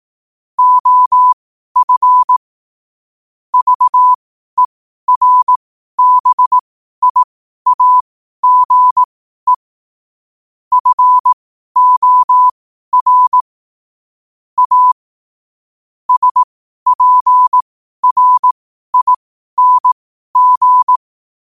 Morse code, heard as a keyed tone, about 18 wpm, spaced as for 12 wpm, 1000 Hz. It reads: OF VERBIAGE FOR A SPRING
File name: qotdfiles / 12wpm